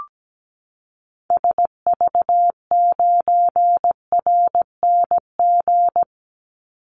Morse code, heard as a keyed tone, about 17 wpm, 700 Hz. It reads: SV9RNG
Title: Call Signs Repeated 3x - 17wpm